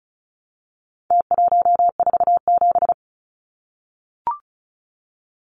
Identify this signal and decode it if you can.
Morse code, heard as a keyed tone, about 34 wpm, 700 Hz. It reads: T147
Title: Numbers - Contest Style - 4-Digit 35wpm